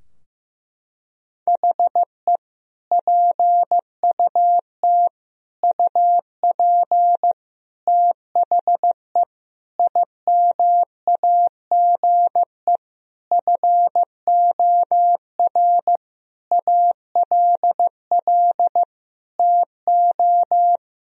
Morse code, heard as a keyed tone, about 15 wpm, 700 Hz. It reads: HE PUT UP THE IMAGE FOR ALL TO
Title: Sentences from Top 1000 Words - Review Part 4 - 15wpm